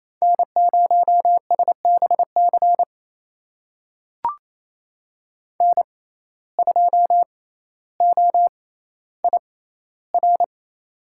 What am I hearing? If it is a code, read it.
Morse code, heard as a keyed tone, about 28 wpm, spaced as for 14 wpm, 700 Hz. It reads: N0SBC N2OIR
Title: USA Call Signs - 5 Characters - 6x Character Spacing - 28wpm